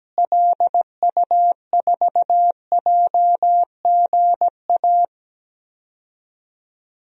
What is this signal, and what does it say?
Morse code, heard as a keyed tone, about 17 wpm, 700 Hz. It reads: LU4JGA
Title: DXCC-100 Call Signs - 6 Characters - Repeated 3x - 17wpm